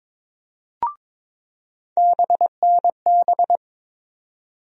Morse code, heard as a keyed tone, about 21 wpm, 700 Hz. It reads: BNB
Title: USA Call Sign Postfixes - 3 Letters 22wpm